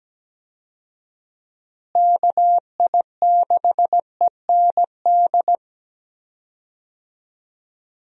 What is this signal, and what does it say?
Morse code, heard as a keyed tone, about 17 wpm, 700 Hz. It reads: KI6END
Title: USA Call Signs - 6 Characters - Repeated 3x - 17wpm